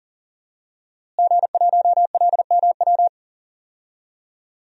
Morse code, heard as a keyed tone, about 40 wpm, 700 Hz. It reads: G1LMW
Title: DXCC-100 Call Signs - 5 Characters - Repeated 2x - 40wpm